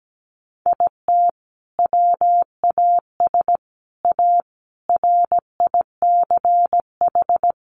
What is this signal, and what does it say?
Morse code, heard as a keyed tone, about 17 wpm, 700 Hz. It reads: IT WAS A RICH
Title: Sentences from Top 1100 Words 17wpm